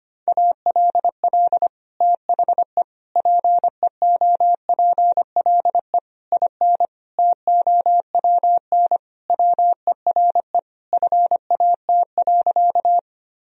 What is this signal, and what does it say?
Morse code, heard as a keyed tone, about 25 wpm, 700 Hz. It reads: ALL THE PEOPLE IN TOWN WERE FAT.
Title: Sentences from Top 1000 Words - Review Part 4 - 25wpm